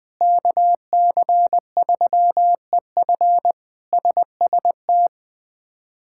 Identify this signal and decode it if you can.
Morse code, heard as a keyed tone, about 21 wpm, 700 Hz. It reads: KC3EF SST
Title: K1USN SST QSO Practice - 20wpm